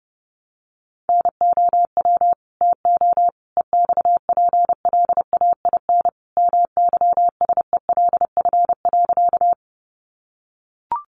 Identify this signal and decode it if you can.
Morse code, heard as a keyed tone, about 30 wpm, 700 Hz. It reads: NOW TO EXPLAIN MYSELF.
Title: Sentences from Top 1300 Words 30wpm